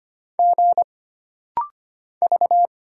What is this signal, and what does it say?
Morse code, heard as a keyed tone, about 25 wpm, 700 Hz. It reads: G V
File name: Single Character (Letter:Number:Punctuation:Prosign) - Rapid-Fire - 25wpm